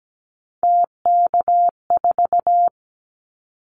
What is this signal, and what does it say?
Morse code, heard as a keyed tone, about 17 wpm, 700 Hz. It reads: TK4
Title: Callsign Prefixes for Top 100 DXCC - 17wpm